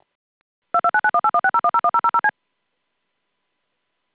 DTMF, heard as telephone touch tones, tones of about 70 ms. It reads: 229918160101000C
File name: ASTERISK-19610 / 00- -SIP-2299- -in